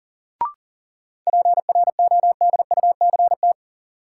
Morse code, heard as a keyed tone, about 39 wpm, 700 Hz. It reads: PRODUCT
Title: Sentences from Top 1400 Words - New Word Drill Alt - 40wpm